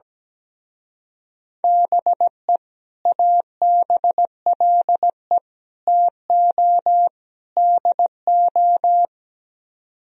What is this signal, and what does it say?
Morse code, heard as a keyed tone, about 17 wpm, 700 Hz. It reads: BE ABLE TO DO
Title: Speed-Racing - Sets of 4 Words 17wpm